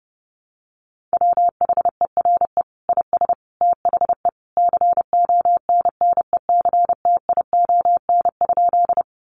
Morse code, heard as a keyed tone, about 30 wpm, 700 Hz. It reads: WHERE IS THE CONNECTION?
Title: Sentences from Top 1300 Words 30wpm